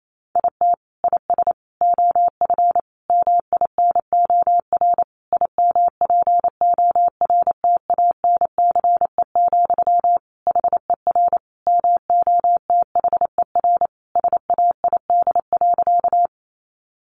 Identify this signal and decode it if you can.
Morse code, heard as a keyed tone, about 28 wpm, 700 Hz. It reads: IT IS OF MINOR IMPORTANCE, HER MOTHER SAID.
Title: Sentences from Top 1500 Words 28wpm